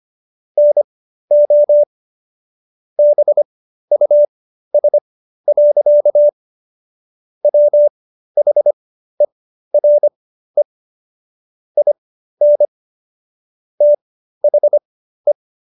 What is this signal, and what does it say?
Morse code, heard as a keyed tone, about 25 wpm, spaced as for 13 wpm, 590 Hz. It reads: NO BUS. WHERE IN THE